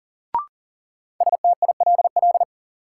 Morse code, heard as a keyed tone, about 40 wpm, 700 Hz. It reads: STILL
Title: Sentences from Top 300 Words - New Word Drill - 40wpm